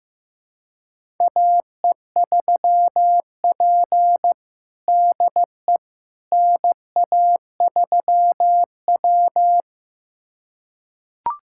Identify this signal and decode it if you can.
Morse code, heard as a keyed tone, about 15 wpm, 700 Hz. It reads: AE3P DE NA3W
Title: USA Callsign DE Callsign - 4 Characters - 15wpm